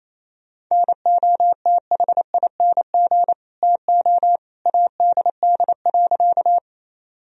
Morse code, heard as a keyed tone, about 28 wpm, 700 Hz. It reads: NOTHING TO ADD.
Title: Sentences from Top 1000 Words - Review Part 1 - 28wpm